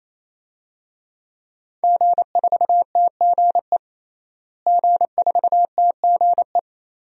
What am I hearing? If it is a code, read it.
Morse code, heard as a keyed tone, about 28 wpm, 700 Hz. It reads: G4TGE G4TGE
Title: DXCC-100 Call Signs - 5 Characters - Repeated 3x - 28wpm